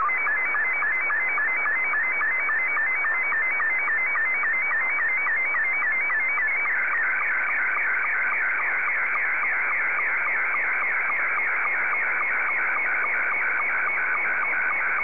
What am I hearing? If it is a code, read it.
SSTV (slow-scan television), Scottie 2.